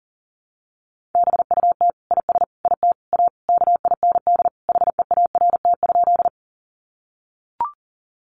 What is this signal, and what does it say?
Morse code, heard as a keyed tone, about 40 wpm, 700 Hz. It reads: BUT IS IT A KIND HEART?
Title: Sentences from Top 1200 Words 40wpm